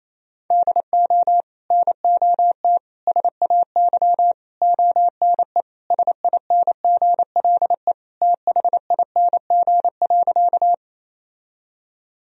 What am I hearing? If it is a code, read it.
Morse code, heard as a keyed tone, about 28 wpm, 700 Hz. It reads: DO NOT SAY ONE SINGLE THING.
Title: Sentences from Top 1000 Words - Review Part 4 - 28wpm